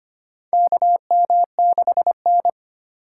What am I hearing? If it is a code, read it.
Morse code, heard as a keyed tone, about 25 wpm, 700 Hz. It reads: KM6N
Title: USA Call Signs - 4 Characters - 2x Character Spacing - 25wpm